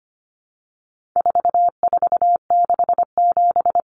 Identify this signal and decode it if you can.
Morse code, heard as a keyed tone, about 25 wpm, 700 Hz. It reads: 4467
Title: Numbers - 4-Digit 25wpm